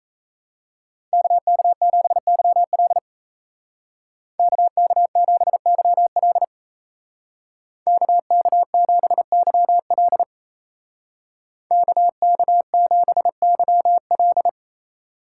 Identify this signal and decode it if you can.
Morse code, heard as a keyed tone, about 33 wpm, 700 Hz. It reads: KK7YL KK7YL KK7YL KK7YL
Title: Speed Racing - USA 2x2 Call Signs 28wpm